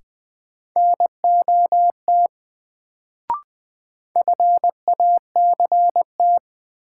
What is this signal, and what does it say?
Morse code, heard as a keyed tone, about 20 wpm, 700 Hz. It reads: NOT FACT
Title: Sets of 2 Words - New Word Drill - 20wpm